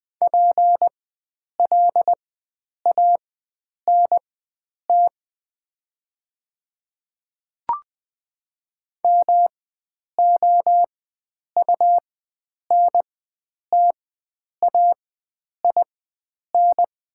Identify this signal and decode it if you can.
Morse code, heard as a keyed tone, about 20 wpm, spaced as for 9 wpm, 700 Hz. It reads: PLANT MOUNTAIN
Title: 4x Character Spacing - Top 500 Words - 20wpm